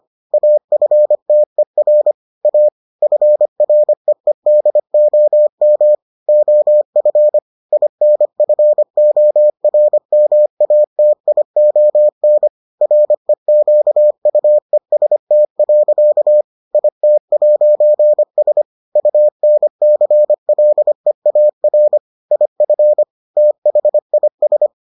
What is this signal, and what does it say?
Morse code, heard as a keyed tone, about 25 wpm, 600 Hz. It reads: AFTER A FREEDOM OF INFORMATION REQUEST. IT'S UNCLEAR IF THIS